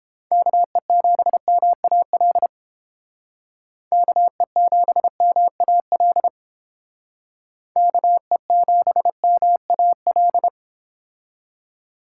Morse code, heard as a keyed tone, about 30 wpm, 700 Hz. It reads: KE7MAL KE7MAL KE7MAL